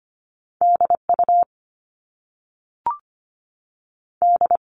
Morse code, heard as a keyed tone, about 25 wpm, 700 Hz. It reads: DU D
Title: All Cut Numbers - 2-Digit 25wpm